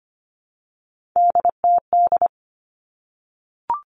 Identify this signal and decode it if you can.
Morse code, heard as a keyed tone, about 25 wpm, 700 Hz. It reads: DTD
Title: Three Challenging Letter-Number Combinations 25wpm